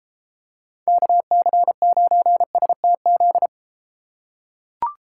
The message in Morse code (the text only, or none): KC9STZ